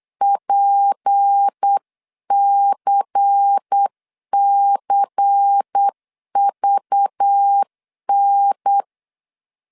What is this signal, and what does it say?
Morse code, heard as a keyed tone, about 8 wpm, 800 Hz. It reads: PCCVN
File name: CW / DET / 1 6 1500C S C 8wpm 5219